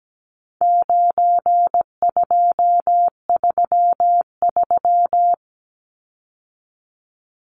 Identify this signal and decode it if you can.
Morse code, heard as a keyed tone, about 17 wpm, 700 Hz. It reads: 9233
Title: Numbers - 4-Digit 17wpm